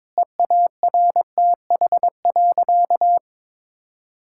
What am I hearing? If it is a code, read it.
Morse code, heard as a keyed tone, about 22 wpm, 700 Hz. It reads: EARTH.
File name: Difficult Sentences with Hints - 22wpm